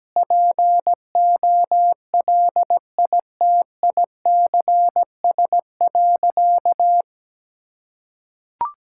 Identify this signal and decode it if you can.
Morse code, heard as a keyed tone, about 17 wpm, 700 Hz. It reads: POLITICS.
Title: Sentences from Top 1000 Words - Review Part 6 - 17wpm